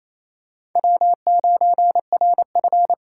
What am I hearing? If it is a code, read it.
Morse code, heard as a keyed tone, about 28 wpm, 700 Hz. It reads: W9RF
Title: USA Call Signs - 4 Characters - 3x Character Spacing - 28wpm